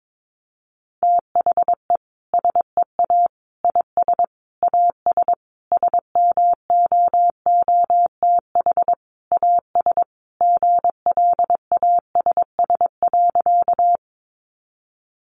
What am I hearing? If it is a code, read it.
Morse code, heard as a keyed tone, about 22 wpm, 700 Hz. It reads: THE SEA IS AS SMOOTH AS GLASS.
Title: Sentences from Top 1400 Words 22wpm